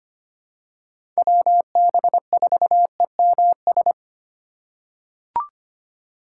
Morse code, heard as a keyed tone, about 25 wpm, 700 Hz. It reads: WB4EMS
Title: USA Call Signs - 6 Characters - 3x Character Spacing - 25wpm